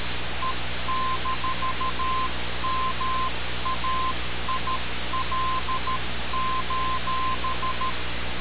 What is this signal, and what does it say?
Morse code, heard as a keyed tone, about 13 wpm, 1040 Hz, with noise. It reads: E-MAIL:
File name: radiocap / REC z